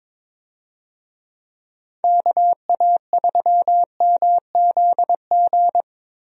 Morse code, heard as a keyed tone, about 22 wpm, 700 Hz. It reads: KA3MZG
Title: USA Call Signs - 6 Characters - Repeated 3x - 22wpm